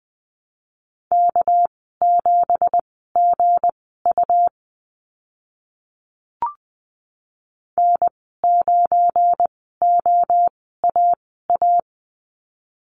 Morse code, heard as a keyed tone, about 20 wpm, spaced as for 15 wpm, 700 Hz. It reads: K7GU N9OAA
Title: 200 Lesson 32g - Letter k - Call Signs 20wpm